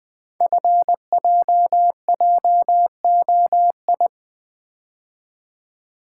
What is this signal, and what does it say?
Morse code, heard as a keyed tone, about 20 wpm, 700 Hz. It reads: FJJOI
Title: Code Groups - 5 Letters - 20wpm